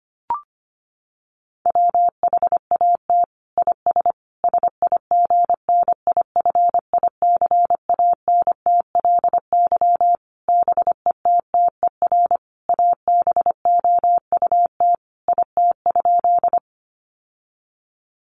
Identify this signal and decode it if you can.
Morse code, heard as a keyed tone, about 25 wpm, 700 Hz. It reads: WHAT IS SIGNIFICANTLY BETTER ABOUT IT?
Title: Sentences from Top 1500 Words 25wpm